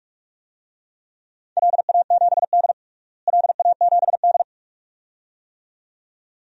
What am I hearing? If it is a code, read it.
Morse code, heard as a keyed tone, about 43 wpm, 700 Hz. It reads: LA7D LA7D
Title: DXCC-100 Call Signs - 4 Characters - Repeated 2x - 45wpm